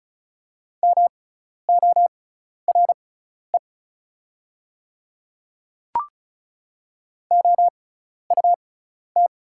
Morse code, heard as a keyed tone, about 36 wpm, spaced as for 12 wpm, 700 Hz. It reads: MORE OUT
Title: 6x Character Spacing - Top 100 Words - 35wpm